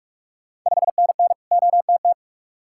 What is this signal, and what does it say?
Morse code, heard as a keyed tone, about 41 wpm, 700 Hz. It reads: 5NN OTT
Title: Ontario QSO Party Exchange 45wpm